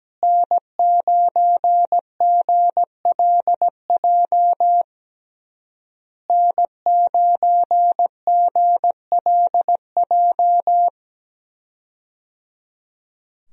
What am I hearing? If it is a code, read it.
Morse code, heard as a keyed tone, about 17 wpm, 700 Hz. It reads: N9GLJ N9GLJ
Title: USA Call Signs - 5 Characters - Repeated 3x - 17wpm